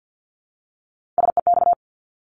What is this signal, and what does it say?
Morse code, heard as a keyed tone, about 50 wpm, 700 Hz. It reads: SEX
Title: Multi-letter Phonemes Repeated 3x 50wpm